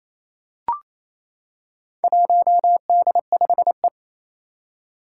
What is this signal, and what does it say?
Morse code, heard as a keyed tone, about 27 wpm, 700 Hz. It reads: 1D5E
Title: Four Challenging Letter-Number Combinations 28wpm